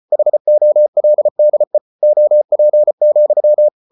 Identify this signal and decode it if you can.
Morse code, heard as a keyed tone, about 34 wpm, 600 Hz. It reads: HOLDE OP,
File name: Ferie QRQ 34wpm